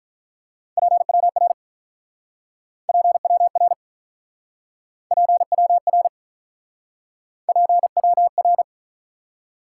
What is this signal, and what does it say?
Morse code, heard as a keyed tone, about 45 wpm, 700 Hz. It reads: PWR PWR PWR PWR